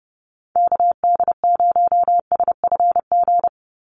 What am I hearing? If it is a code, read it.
Morse code, heard as a keyed tone, about 30 wpm, 700 Hz. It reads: KD0SFG
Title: USA Call Signs 30wpm